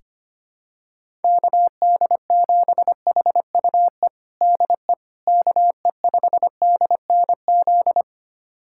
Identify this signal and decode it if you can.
Morse code, heard as a keyed tone, about 25 wpm, 700 Hz. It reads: KD7HUE DE KE5DNZ
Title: USA Callsign DE Callsign - 6 Characters - 25wpm